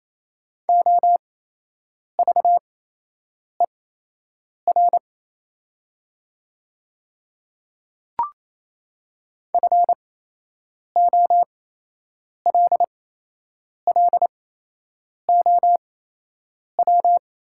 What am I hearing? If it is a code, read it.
Morse code, heard as a keyed tone, about 28 wpm, spaced as for 8 wpm, 700 Hz. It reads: OVER FOLLOW
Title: 8x Character Spacing - Top 500 Words - 28wpm